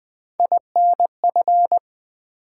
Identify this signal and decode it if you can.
Morse code, heard as a keyed tone, about 20 wpm, 700 Hz. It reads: INF
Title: Build A Word - 3 Letters At A Time - 6-Letter Words - 20wpm